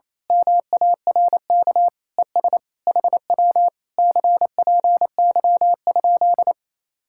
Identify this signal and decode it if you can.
Morse code, heard as a keyed tone, about 27 wpm, 700 Hz. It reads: MARK ES HW CPY?
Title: Speed-Racing Common QSO Phrases Encore - 28wpm